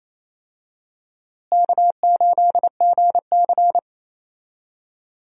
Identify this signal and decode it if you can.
Morse code, heard as a keyed tone, about 28 wpm, 700 Hz. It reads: K8GC
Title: USA Call Signs - 4 Characters - Repeated 3x - 28wpm